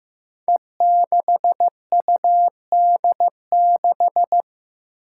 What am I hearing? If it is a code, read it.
Morse code, heard as a keyed tone, about 15 wpm, 700 Hz. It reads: E6UD6
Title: Five Challenging Letter-Number Combinations 15wpm